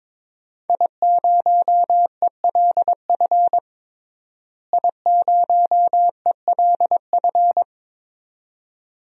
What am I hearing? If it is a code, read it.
Morse code, heard as a keyed tone, about 22 wpm, 700 Hz. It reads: I0ELF I0ELF
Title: DXCC-100 Call Signs - 5 Characters - Repeated 2x - 22wpm